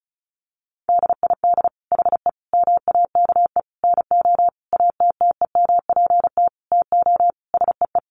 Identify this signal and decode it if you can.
Morse code, heard as a keyed tone, about 34 wpm, 700 Hz. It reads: DID HE MAKE NO ATTEMPT TO SEE
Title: Sentences from Top 1000 Words 35wpm